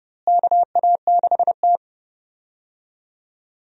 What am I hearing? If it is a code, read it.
Morse code, heard as a keyed tone, about 32 wpm, 700 Hz. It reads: KA6T
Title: USA Call Signs - 4 Characters - Repeated 2x - 30wpm